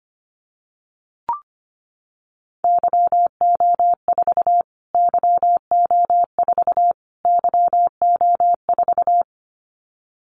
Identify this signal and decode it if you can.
Morse code, heard as a keyed tone, about 25 wpm, 700 Hz. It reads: YO4 YO4 YO4
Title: Callsign Prefixes for Top 25 DXCC - Repeated 3x - 25wpm